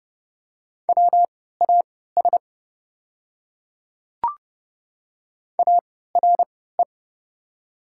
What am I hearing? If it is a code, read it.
Morse code, heard as a keyed tone, about 31 wpm, spaced as for 17 wpm, 700 Hz. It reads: WAS ARE
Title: 3x Character Spacing - Top 200 Words - 30wpm